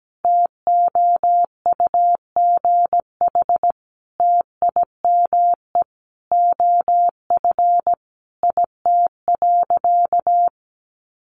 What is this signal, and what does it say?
Morse code, heard as a keyed tone, about 17 wpm, 700 Hz. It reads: TOUGH TIME OF IT.
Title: Sentences from Top 1200 Words 17wpm